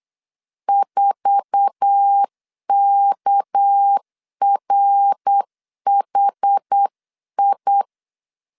Morse code, heard as a keyed tone, about 8 wpm, 800 Hz. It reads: 4KRHI
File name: CW / DET / 1 9 1500C S A 8wpm 5637